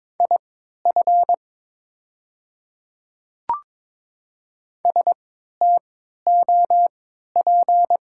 Morse code, heard as a keyed tone, about 22 wpm, spaced as for 12 wpm, 700 Hz. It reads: IF STOP
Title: 3x Character Spacing - Top 500 Words - 22wpm